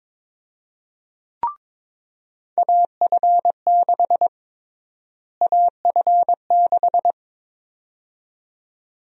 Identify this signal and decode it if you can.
Morse code, heard as a keyed tone, about 22 wpm, 700 Hz. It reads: AF6 AF6
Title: USA Call Sign Prefixes - Repeated 2x - 22wpm